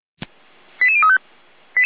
Five-tone selective call (ZVEI), heard as quick runs of five tones, tones of about 70 ms. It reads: 88035